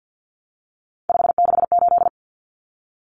Morse code, heard as a keyed tone, about 48 wpm, 700 Hz. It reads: 568
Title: Common Cut Numbers - 3-Digit 50wpm